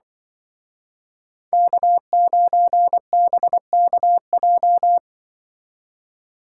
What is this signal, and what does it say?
Morse code, heard as a keyed tone, about 23 wpm, 700 Hz. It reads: K9BKJ
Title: Speed Racing - USA 1x3 Call Signs 20wpm